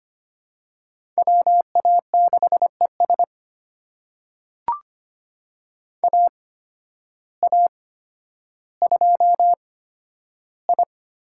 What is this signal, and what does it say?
Morse code, heard as a keyed tone, about 25 wpm, spaced as for 12 wpm, 700 Hz. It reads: WA6ES AA2I
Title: USA Call Signs - 5 Characters - 8x Character Spacing - 25wpm